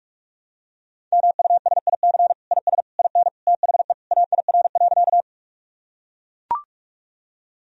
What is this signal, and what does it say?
Morse code, heard as a keyed tone, about 42 wpm, 700 Hz. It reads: MUSIC IS IN THE AIR.
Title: Sentences from Top 1000 Words - Review Part 1 - 45wpm